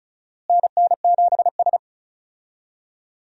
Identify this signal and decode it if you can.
Morse code, heard as a keyed tone, about 36 wpm, 700 Hz. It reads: NN7S
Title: USA Call Signs - 4 Characters - Repeated 2x - 35wpm